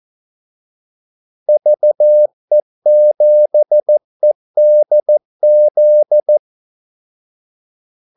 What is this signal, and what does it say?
Morse code, heard as a keyed tone, about 14 wpm, 600 Hz. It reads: VE7EDZ
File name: Callsigns-14wpm